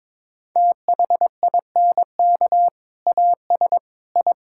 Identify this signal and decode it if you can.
Morse code, heard as a keyed tone, about 22 wpm, 700 Hz. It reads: THINK AS I
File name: practice / 22wpm / Sentences with 2-Word Phrases - 22wpm